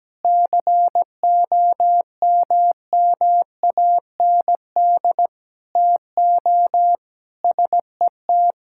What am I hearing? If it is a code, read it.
Morse code, heard as a keyed tone, about 17 wpm, 700 Hz. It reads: COMMAND TO SET
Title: Sentences from Top 1600 Words 17wpm